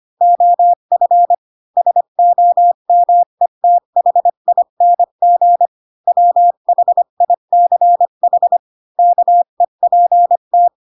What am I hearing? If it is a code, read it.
Morse code, heard as a keyed tone, about 25 wpm, 700 Hz. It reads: OF SOMETHING WHICH KEPT
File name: Pinocchio / Pinocchio ch3 20 25 ik1hge